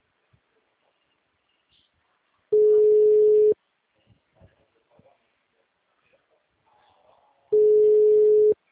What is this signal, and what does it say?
Ringback tone (Europe, 425 Hz), with speech.